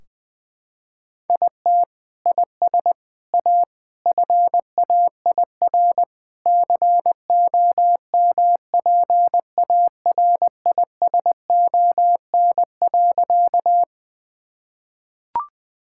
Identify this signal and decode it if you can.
Morse code, heard as a keyed tone, about 20 wpm, 700 Hz. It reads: IT IS A FAIR COMPARISON.
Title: Sentences from Top 1600 Words 20wpm